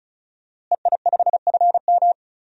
Morse code, heard as a keyed tone, about 36 wpm, 700 Hz. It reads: EI5FM
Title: DXCC-100 Call Signs - 5 Characters - Repeated 2x - 35wpm